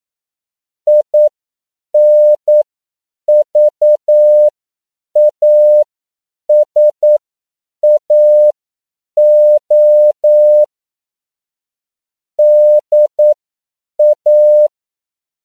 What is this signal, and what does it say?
Morse code, heard as a keyed tone, about 9 wpm, spaced as for 7 wpm, 600 Hz. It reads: INVASAO DA
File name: CW / 10PPM